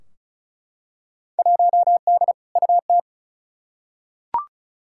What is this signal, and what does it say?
Morse code, heard as a keyed tone, about 34 wpm, 700 Hz. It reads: 1D UT
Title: Field Day Exchanges - 35wpm